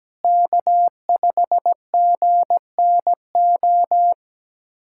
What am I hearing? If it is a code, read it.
Morse code, heard as a keyed tone, about 17 wpm, 700 Hz. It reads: K5GNO
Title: USA Callsign DE Callsign - 5 Characters - 17wpm